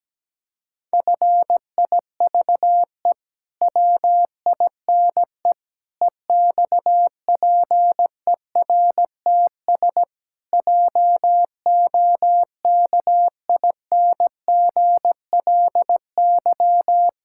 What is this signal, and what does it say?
Morse code, heard as a keyed tone, about 17 wpm, 700 Hz. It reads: FIVE WINE EXPERTS JOKINGLY
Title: Pangram Challenge v2 - 17wpm